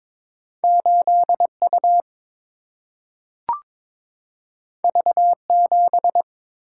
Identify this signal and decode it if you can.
Morse code, heard as a keyed tone, about 22 wpm, 700 Hz. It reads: 8 U V 7
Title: Single-Letter Single-Number 22wpm